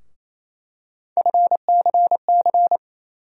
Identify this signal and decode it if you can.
Morse code, heard as a keyed tone, about 28 wpm, 700 Hz. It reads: FCC
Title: USA Call Sign Postfixes - 3 Letters 28wpm